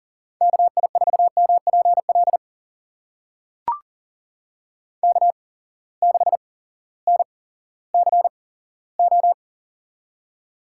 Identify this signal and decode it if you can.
Morse code, heard as a keyed tone, about 38 wpm, spaced as for 18 wpm, 700 Hz. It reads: KI4MPL K6NCO
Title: USA Call Signs - 8x Character Spacing - 40wpm